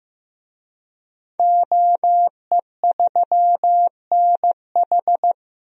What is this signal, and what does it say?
Morse code, heard as a keyed tone, about 15 wpm, 700 Hz. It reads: OE3NH
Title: DXCC-100 Call Signs - 5 Characters - Repeated 2x - 15wpm